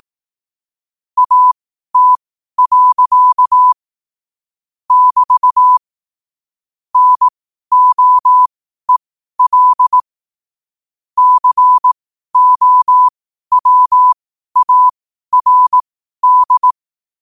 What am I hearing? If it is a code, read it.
Morse code, heard as a keyed tone, about 18 wpm, spaced as for 12 wpm, 1000 Hz. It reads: AT. <BT> NOEL COWARD
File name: qotdfiles / 12wpm